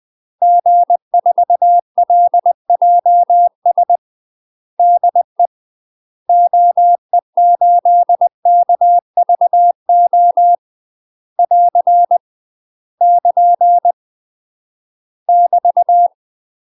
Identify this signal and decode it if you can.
Morse code, heard as a keyed tone, about 20 wpm, 700 Hz. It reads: G4LJS DE OE8KVO <AR> <KN> <BT>
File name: UK MASTER QSOs TRAINING / 20WPM / UK MASTER QSOs 0038